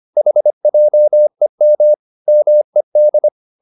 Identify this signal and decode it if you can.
Morse code, heard as a keyed tone, about 25 wpm, 600 Hz. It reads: HJEM MED
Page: Ferie 25 wpm